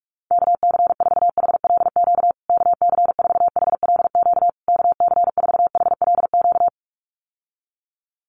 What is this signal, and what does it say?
Morse code, heard as a keyed tone, about 44 wpm, 700 Hz. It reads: KC4HLQ KC4HLQ KC4HLQ
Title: USA Call Signs Repeated 3x 45wpm